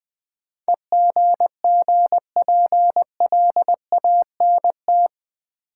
Morse code, heard as a keyed tone, about 20 wpm, 700 Hz. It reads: EGGPLANT
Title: Easy Words with Hints - 20wpm